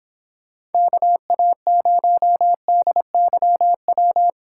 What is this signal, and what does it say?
Morse code, heard as a keyed tone, about 31 wpm, 700 Hz. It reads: KA0DYW